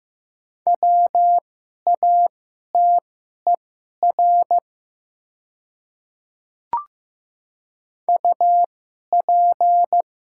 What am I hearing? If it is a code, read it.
Morse code, heard as a keyed tone, about 15 wpm, spaced as for 11 wpm, 700 Hz. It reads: WATER UP
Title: 2x Character Spacing - Top 100 Words - 15wpm